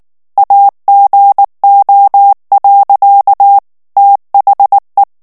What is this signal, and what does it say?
Morse code, heard as a keyed tone, about 19 wpm, 800 Hz. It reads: AGO. THE
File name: ANGLAIS texte clair 3 vitesse 18 mots minute